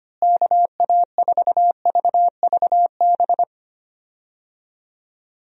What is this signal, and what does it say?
Morse code, heard as a keyed tone, about 25 wpm, 700 Hz. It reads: KA4VVB
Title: USA Call Signs - 6 Characters - Repeated 2x - 25wpm